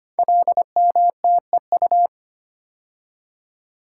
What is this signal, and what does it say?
Morse code, heard as a keyed tone, about 25 wpm, 700 Hz. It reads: LMTEU
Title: Code Groups - 5 Letters - 25wpm